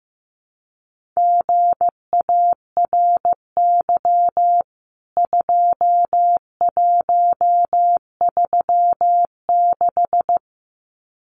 Morse code, heard as a keyed tone, about 15 wpm, 700 Hz. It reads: GARY 2136
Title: CWT Exchange 15wpm